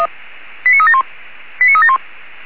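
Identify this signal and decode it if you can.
Five-tone selective call (ZVEI), heard as quick runs of five tones, tones of about 70 ms. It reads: 89371 89371
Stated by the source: ZVEI five-tone selective call received on SDR.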